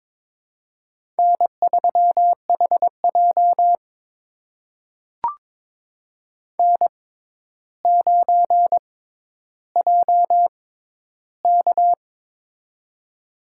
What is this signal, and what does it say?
Morse code, heard as a keyed tone, about 22 wpm, spaced as for 11 wpm, 700 Hz. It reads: N3HJ N9JK
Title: USA Call Signs - 4 Characters - 6x Character Spacing - 22wpm